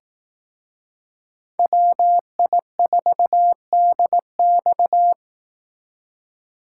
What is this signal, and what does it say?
Morse code, heard as a keyed tone, about 18 wpm, 700 Hz. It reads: WI4DX
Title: Speed Racing - USA Call Signs 15wpm